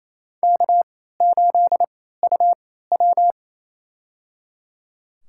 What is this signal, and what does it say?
Morse code, heard as a keyed tone, about 28 wpm, spaced as for 16 wpm, 700 Hz. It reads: K8UW
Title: USA Call Signs - 4 Characters - 3x Character Spacing - 28wpm